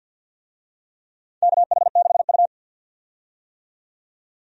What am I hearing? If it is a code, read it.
Morse code, heard as a keyed tone, about 47 wpm, 700 Hz. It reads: KH6U
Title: DXCC-100 Call Signs - 4 Characters - 50wpm